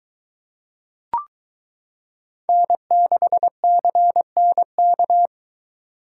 Morse code, heard as a keyed tone, about 22 wpm, 700 Hz. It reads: N6CNK